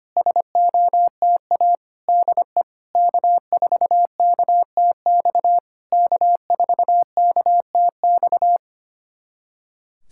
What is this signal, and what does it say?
Morse code, heard as a keyed tone, about 25 wpm, 700 Hz. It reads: SOTA DE K4KTX K4KTX
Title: SOTA QSO Practice 25wpm